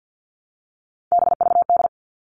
Morse code, heard as a keyed tone, about 50 wpm, 700 Hz. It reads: BVD
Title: All Cut Numbers - 3-Digit 50wpm